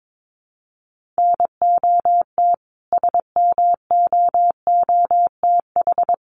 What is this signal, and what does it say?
Morse code, heard as a keyed tone, about 22 wpm, 700 Hz. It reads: NOT SMOOTH
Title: Sentences from Top 1400 Words 22wpm